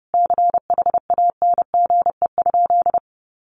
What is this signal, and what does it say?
Morse code, heard as a keyed tone, about 30 wpm, 700 Hz. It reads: CHANGE?
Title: Sentences from Top 1500 Words 30wpm